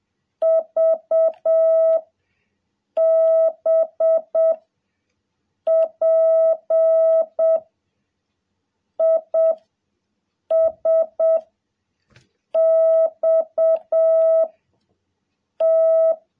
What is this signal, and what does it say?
Morse code, heard as a keyed tone, about 7 wpm, 660 Hz. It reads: V B P I S X T
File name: jh8cbh / 2021cw